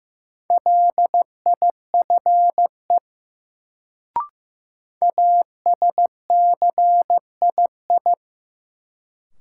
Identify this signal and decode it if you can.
Morse code, heard as a keyed tone, about 15 wpm, 700 Hz. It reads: LIFE ASCII
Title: Sets of 5 Words - New Word Drill - 15wpm